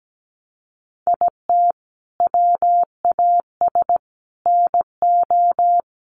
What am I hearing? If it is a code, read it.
Morse code, heard as a keyed tone, about 17 wpm, 700 Hz. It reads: IT WAS NO
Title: Sentences from Top 1500 Words 17wpm